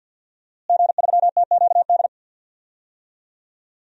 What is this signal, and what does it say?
Morse code, heard as a keyed tone, about 51 wpm, 700 Hz. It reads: G3TQD
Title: DXCC-100 Call Signs - 5 Characters - Repeated 3x - 50wpm